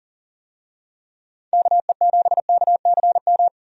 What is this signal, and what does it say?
Morse code, heard as a keyed tone, about 40 wpm, 700 Hz. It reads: KE7KCM
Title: USA Call Signs - 6 Characters - 40wpm